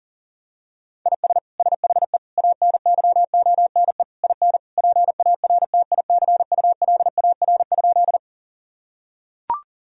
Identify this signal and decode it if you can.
Morse code, heard as a keyed tone, about 40 wpm, 700 Hz. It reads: IS SHE ANYONE IN PARTICULAR?
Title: Sentences from Top 1000 Words - Review Part 4 - 40wpm